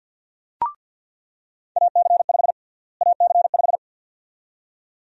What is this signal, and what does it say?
Morse code, heard as a keyed tone, about 46 wpm, 700 Hz. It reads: AC5 AC5
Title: USA Call Sign Prefixes - Repeated 2x - 50wpm